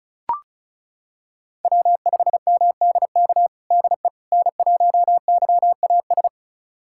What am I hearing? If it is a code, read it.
Morse code, heard as a keyed tone, about 36 wpm, 700 Hz. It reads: W5MDK DE N1YAS
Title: USA Callsign DE Callsign - 5 Characters - 35wpm